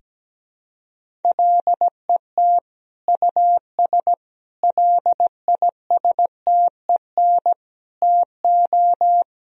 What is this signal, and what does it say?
Morse code, heard as a keyed tone, about 17 wpm, 700 Hz. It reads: LET US LISTEN TO
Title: Sentences from Top 1000 Words - Review Part 3 - 17wpm